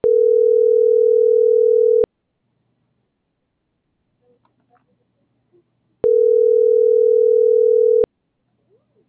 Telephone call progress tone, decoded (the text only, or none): ringback tone